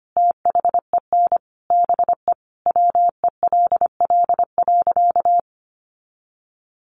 Morse code, heard as a keyed tone, about 25 wpm, 700 Hz. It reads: THEN BE WELL.
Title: Sentences from Top 1500 Words 25wpm